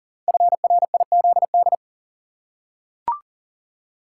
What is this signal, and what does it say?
Morse code, heard as a keyed tone, about 42 wpm, 700 Hz. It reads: FRIZD
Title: Code Groups - 5 Letter:Numbers - 40wpm